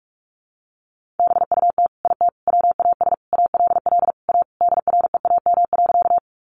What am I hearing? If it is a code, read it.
Morse code, heard as a keyed tone, about 44 wpm, 700 Hz. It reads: BUT IT WAS ALL A DREAM.
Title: Sentences from Top 1500 Words 45wpm